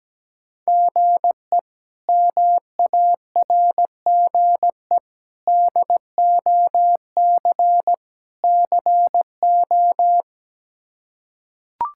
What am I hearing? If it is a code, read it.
Morse code, heard as a keyed tone, about 17 wpm, 700 Hz. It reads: GE MARGE DOC CO
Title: K1USN SST QSO Practice - 17wpm